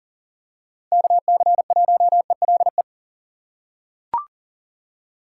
Morse code, heard as a keyed tone, about 39 wpm, 690 Hz. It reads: KC1ELE
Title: USA Call Signs - 6x Character Spacing - 40wpm